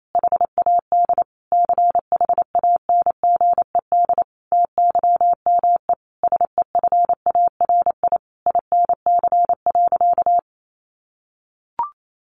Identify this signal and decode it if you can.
Morse code, heard as a keyed tone, about 28 wpm, 700 Hz. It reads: HAD CHANGED TYME SEFARI INC.
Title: A Sound Of Thunder 28wpm